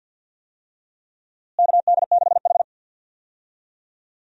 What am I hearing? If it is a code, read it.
Morse code, heard as a keyed tone, about 51 wpm, 700 Hz. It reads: KD6H
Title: USA Call Signs - 4 Characters - Repeated 3x - 50wpm